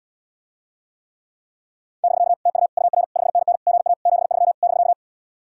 Morse code, heard as a keyed tone, about 79 wpm, 700 Hz. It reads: <BK> TU UR 5NN GA 73 <BK>